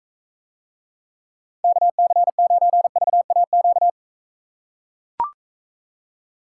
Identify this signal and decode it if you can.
Morse code, heard as a keyed tone, about 45 wpm, 700 Hz. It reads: KC9VAQ